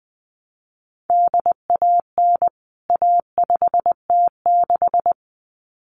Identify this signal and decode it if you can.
Morse code, heard as a keyed tone, about 20 wpm, 700 Hz. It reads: DAN A5T6
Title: CWT Exchange 20wpm